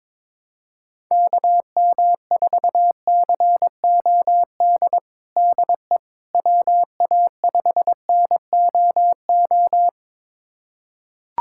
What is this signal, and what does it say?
Morse code, heard as a keyed tone, about 22 wpm, 700 Hz. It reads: KM4COD DE WA5NOO
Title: USA Callsign DE Callsign - 6 Characters - 22wpm